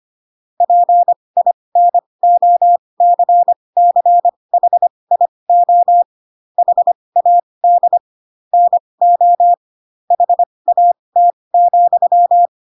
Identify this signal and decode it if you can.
Morse code, heard as a keyed tone, about 25 wpm, spaced as for 20 wpm, 700 Hz. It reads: PINOCCHIO HAD NO HAT,
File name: Pinocchio / Pinocchio ch6 15 25 ik1hge